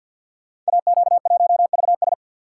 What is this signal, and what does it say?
Morse code, heard as a keyed tone, about 53 wpm, 700 Hz. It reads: AQ1VS
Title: Code Groups - 5 Letter:Numbers - 50wpm